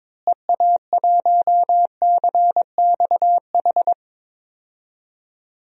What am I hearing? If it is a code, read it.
Morse code, heard as a keyed tone, about 22 wpm, 700 Hz. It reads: EA1CXH
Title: DXCC-100 Call Signs - 6 Characters - Repeated 3x - 22wpm